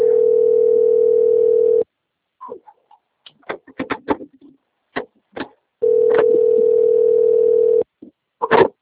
Ringback tone (North America), 440 + 480 Hz.
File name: out- - - - - .76707